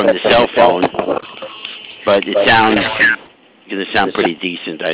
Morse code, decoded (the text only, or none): none